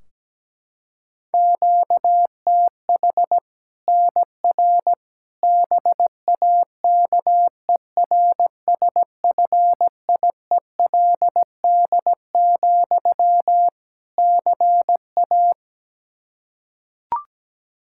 Morse code, heard as a keyed tone, about 17 wpm, 700 Hz. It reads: QTH NR BAKERSFIELD, CA
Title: K1USN SST QSO Practice 17wpm